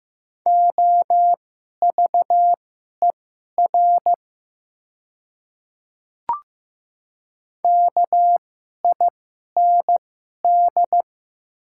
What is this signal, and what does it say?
Morse code, heard as a keyed tone, about 15 wpm, spaced as for 11 wpm, 700 Hz. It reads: OVER KIND
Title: 2x Character Spacing - Top 200 Words - 15wpm